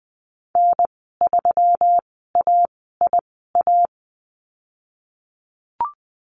Morse code, heard as a keyed tone, about 19 wpm, spaced as for 14 wpm, 700 Hz. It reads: N3AIA
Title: 122 Lesson 20g - Number 3 - Call Signs 20wpm